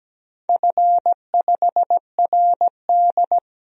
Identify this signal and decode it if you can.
Morse code, heard as a keyed tone, about 17 wpm, 700 Hz. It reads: F5RD
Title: DXCC-100 Call Signs - 4 Characters - Repeated 2x - 17wpm